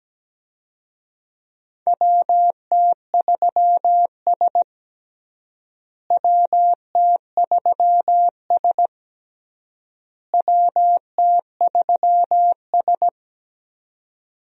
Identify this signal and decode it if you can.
Morse code, heard as a keyed tone, about 17 wpm, 700 Hz. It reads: WT3S WT3S WT3S
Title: USA Call Signs - 4 Characters - Repeated 3x - 17wpm